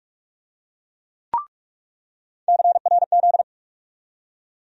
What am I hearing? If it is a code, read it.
Morse code, heard as a keyed tone, about 46 wpm, 700 Hz. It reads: CRZ
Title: USA Call Sign Postfixes - 3 Letters 45wpm